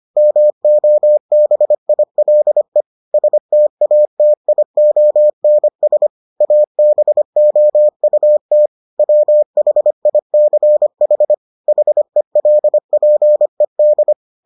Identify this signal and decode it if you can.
Morse code, heard as a keyed tone, about 25 wpm, 600 Hz. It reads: MOBILE STATIONS ABOUT WHICH HELPED